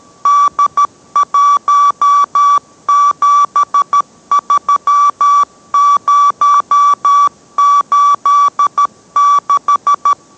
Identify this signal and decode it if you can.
Morse code, heard as a keyed tone, about 13 wpm, 1150 Hz. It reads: D173086